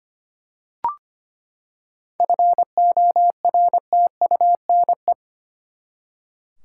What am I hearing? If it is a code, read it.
Morse code, heard as a keyed tone, about 25 wpm, 700 Hz. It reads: FORTUNE
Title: Seven Letter Words - 25wpm